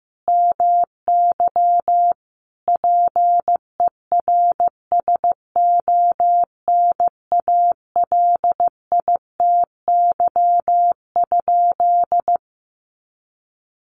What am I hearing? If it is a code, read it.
Morse code, heard as a keyed tone, about 15 wpm, 700 Hz. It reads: MY PERSONALITY?
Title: Sentences from Top 1400 Words 15wpm